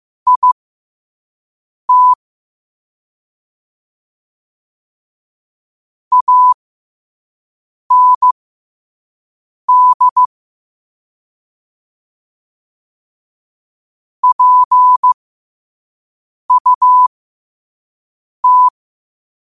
Morse code, heard as a keyed tone, about 15 wpm, spaced as for 5 wpm, 1000 Hz. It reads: IT AND PUT